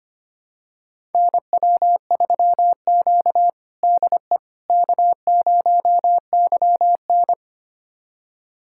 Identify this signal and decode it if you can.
Morse code, heard as a keyed tone, about 25 wpm, 700 Hz. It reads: NW3Q DE K0YN
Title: USA Callsign DE Callsign - 4 Characters - 25wpm